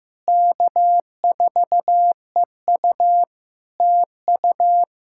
Morse code, heard as a keyed tone, about 14 wpm, 700 Hz. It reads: K4EU TU
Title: Practice QSO - Exchange 4 - Ending - 15wpm